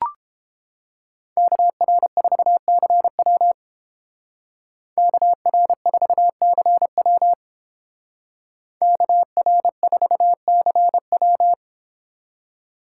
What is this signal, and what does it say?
Morse code, heard as a keyed tone, about 30 wpm, 700 Hz. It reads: KR4CW KR4CW KR4CW